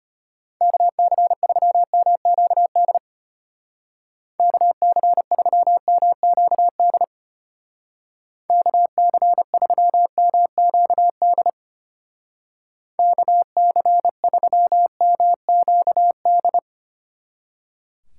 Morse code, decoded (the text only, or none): KC3MQD KC3MQD KC3MQD KC3MQD